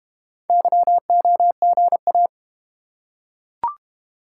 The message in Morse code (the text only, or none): YOGA